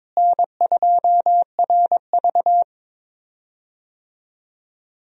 Morse code, heard as a keyed tone, about 22 wpm, 700 Hz. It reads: N2RV
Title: USA Call Signs - 4 Characters - Repeated 3x - 22wpm